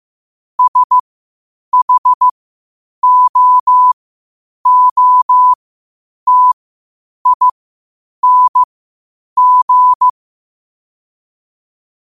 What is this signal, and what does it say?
Morse code, heard as a keyed tone, about 15 wpm, spaced as for 8 wpm, 1000 Hz. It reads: SHOOTING
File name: qotdfiles / 8wpm